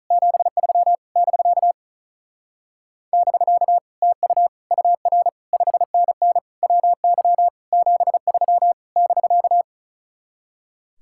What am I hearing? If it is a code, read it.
Morse code, heard as a keyed tone, about 35 wpm, 700 Hz. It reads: 73 <BK> <BK> TU UR 5NN WY 73 <BK>
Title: Speed-Racing Common QSO Phrases - 35wpm